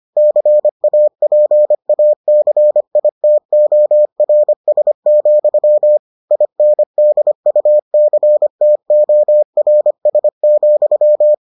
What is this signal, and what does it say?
Morse code, heard as a keyed tone, about 25 wpm, 600 Hz. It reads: CAPACITORS, INDUCTORS,